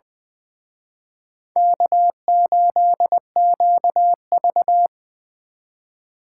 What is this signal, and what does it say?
Morse code, heard as a keyed tone, about 25 wpm, 700 Hz. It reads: K8QV
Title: Speed Racing - USA 1x2 Call Signs 20wpm